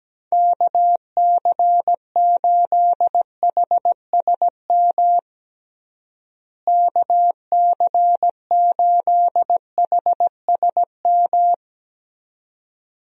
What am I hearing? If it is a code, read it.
Morse code, heard as a keyed tone, about 17 wpm, 700 Hz. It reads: KC8HSM KC8HSM
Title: USA Call Signs - 6 Characters - Repeated 2x - 17wpm